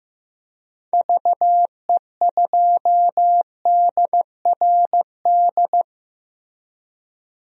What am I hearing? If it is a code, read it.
Morse code, heard as a keyed tone, about 15 wpm, 700 Hz. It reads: VE2DRD
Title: DXCC-100 Call Signs - 6 Characters - Repeated 3x - 15wpm